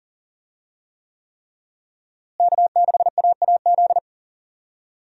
Morse code, heard as a keyed tone, about 40 wpm, 700 Hz. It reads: K6AAZ
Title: USA Call Signs - 5 Characters - Repeated 3x - 40wpm